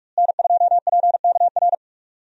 Morse code, heard as a keyed tone, about 46 wpm, 700 Hz. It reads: N2PKR
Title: USA Call Signs - 5 Characters - Repeated 3x - 45wpm